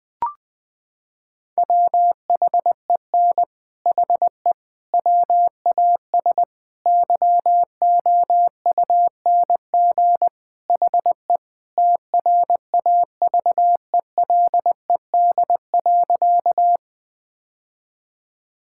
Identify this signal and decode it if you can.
Morse code, heard as a keyed tone, about 20 wpm, 700 Hz. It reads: WHEN HE WAS YOUNG HE TRAVELED.
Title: Sentences with 3-Word Phrases - 20wpm